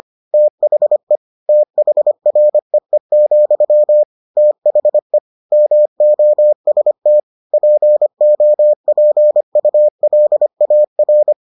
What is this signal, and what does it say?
Morse code, heard as a keyed tone, about 25 wpm, 600 Hz. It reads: THE THREE, THE MOST POPULAR